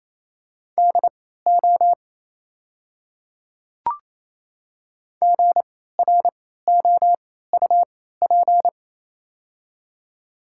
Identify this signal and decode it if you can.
Morse code, heard as a keyed tone, about 28 wpm, spaced as for 16 wpm, 690 Hz. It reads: DO GROUP
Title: 3x Character Spacing - Top 500 Words - 28wpm